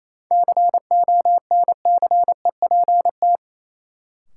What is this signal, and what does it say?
Morse code, heard as a keyed tone, about 28 wpm, 700 Hz. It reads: CONCEPT
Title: Sentences from Top 1100 Words - New Word Drill - 28wpm